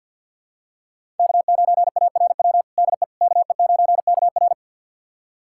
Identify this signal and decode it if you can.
Morse code, heard as a keyed tone, about 51 wpm, 700 Hz. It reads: K9ARW DE KE9KR
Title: USA Callsign DE Callsign - 5 Characters - 50wpm